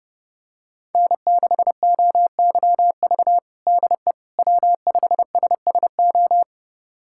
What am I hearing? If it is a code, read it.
Morse code, heard as a keyed tone, about 30 wpm, 700 Hz. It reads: N6OYV DE W5SSO
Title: USA Callsign DE Callsign - 5 Characters - 30wpm